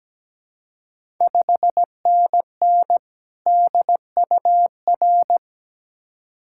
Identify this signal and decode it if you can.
Morse code, heard as a keyed tone, about 17 wpm, 700 Hz. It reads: 5NN DUR
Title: Ontario QSO Party Exchange 17wpm